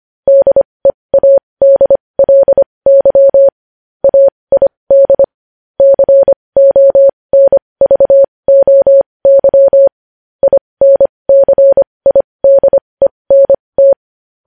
Morse code, heard as a keyed tone, about 25 wpm, spaced as for 20 wpm, 550 Hz. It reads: DEADLY AID CONVOY INCIDENT
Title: News Headlines at 20